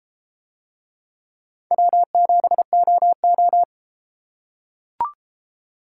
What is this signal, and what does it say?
Morse code, heard as a keyed tone, about 33 wpm, 700 Hz. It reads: W7OO